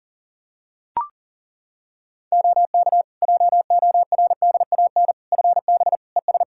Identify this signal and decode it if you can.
Morse code, heard as a keyed tone, about 40 wpm, 700 Hz. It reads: OK JORDAN FB ES
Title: Practice QSO - Exchange 2 - Info - 40wpm